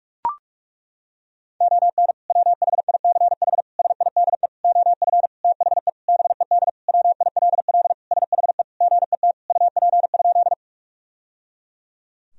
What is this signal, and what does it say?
Morse code, heard as a keyed tone, about 41 wpm, 700 Hz. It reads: ON WHICH SIDE OF THE BED WILL SHE GET UP?
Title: Sentences from Top 1000 Words - Review Part 1 - 45wpm